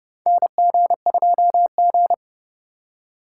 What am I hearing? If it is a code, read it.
Morse code, heard as a keyed tone, about 30 wpm, 700 Hz. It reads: NG2G
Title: USA Call Signs - 4 Characters - 4x Character Spacing - 30wpm